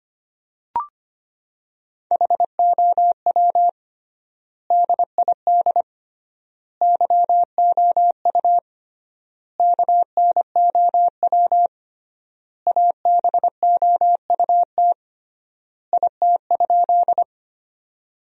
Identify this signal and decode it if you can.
Morse code, heard as a keyed tone, about 25 wpm, 700 Hz. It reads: HOW DID YOU KNOW ABOUT IT?
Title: 3x Word-Spacing - Sentences from Top 100 Words - 25wpm